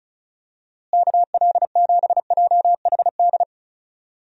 Morse code, heard as a keyed tone, about 34 wpm, 700 Hz. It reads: KL7JHD
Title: USA Call Signs - 6 Characters - Repeated 3x - 35wpm